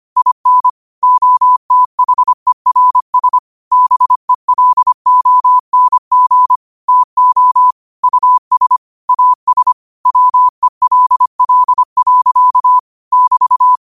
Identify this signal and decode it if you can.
Morse code, heard as a keyed tone, about 25 wpm, 1000 Hz. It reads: IN OTHERS BELONG TO US AS WELL. <BT>
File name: qotdfiles / 25wpm